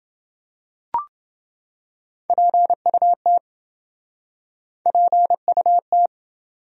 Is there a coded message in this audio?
Yes: Morse code from 2.3 to 6.1 s, heard as a keyed tone, about 29 wpm, 700 Hz. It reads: PUT PUT